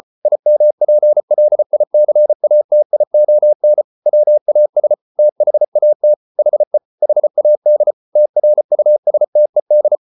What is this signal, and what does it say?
Morse code, heard as a keyed tone, about 34 wpm, 600 Hz. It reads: IMPLICATION WAS THAT HE HAD TRUSTED